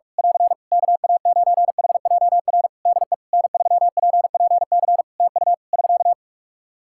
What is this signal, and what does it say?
Morse code, heard as a keyed tone, about 41 wpm, 700 Hz. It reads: <AR> KA9HJR DE N3PPC TU <SK>
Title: Practice QSO - Exchange 4 - Ending - 45wpm